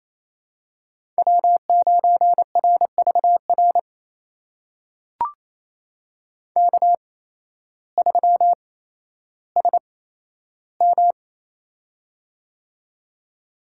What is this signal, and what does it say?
Morse code, heard as a keyed tone, about 28 wpm, spaced as for 13 wpm, 700 Hz. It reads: W9RVR K3SM
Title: USA Call Signs - 8x Character Spacing - 28wpm